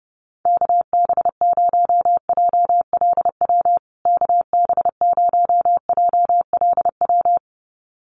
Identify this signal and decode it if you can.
Morse code, heard as a keyed tone, about 30 wpm, 700 Hz. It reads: KB0JLW KB0JLW
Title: USA Call Signs Repeated 2x 30wpm